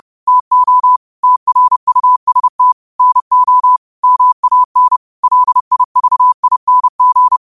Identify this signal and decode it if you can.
Morse code, heard as a keyed tone, about 30 wpm, 1000 Hz. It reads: TO TRUST NO MAN LIVING
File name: qotdfiles / 30wpm